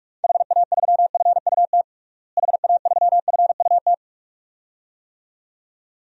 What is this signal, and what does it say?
Morse code, heard as a keyed tone, about 46 wpm, 700 Hz. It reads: HA3FUT HA3FUT
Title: DXCC-100 Call Signs - 6 Characters - Repeated 2x - 45wpm